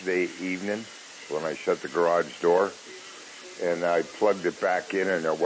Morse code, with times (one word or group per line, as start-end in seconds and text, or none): none